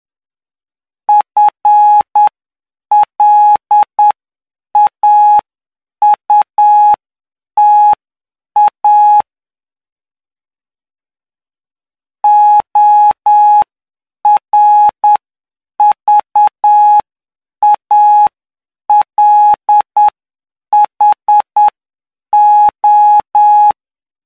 Morse code, heard as a keyed tone, about 9 wpm, spaced as for 7 wpm, 840 Hz. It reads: FLAUTA ORVALHO
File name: cw / py2lcd / 09-PY2LCD-Li��o 18